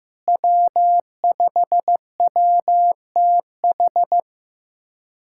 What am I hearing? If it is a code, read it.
Morse code, heard as a keyed tone, about 16 wpm, 700 Hz. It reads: W5WTH
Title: K1USN SST QSO Practice - 15wpm